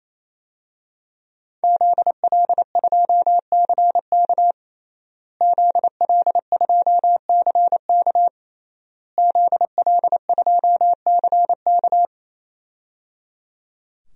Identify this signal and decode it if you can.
Morse code, heard as a keyed tone, about 28 wpm, 700 Hz. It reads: ZL2CK ZL2CK ZL2CK
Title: DXCC-100 Call Signs - 5 Characters - Repeated 3x - 28wpm